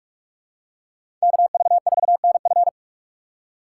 Morse code, heard as a keyed tone, about 46 wpm, 700 Hz. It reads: KV4NF
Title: USA Call Signs - 5 Characters - 4x Character Spacing - 45wpm